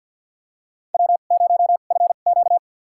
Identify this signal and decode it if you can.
Morse code, heard as a keyed tone, about 48 wpm, spaced as for 35 wpm, 700 Hz. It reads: W0FQ
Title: USA Call Signs - 4 Characters - 2x Character Spacing - 50wpm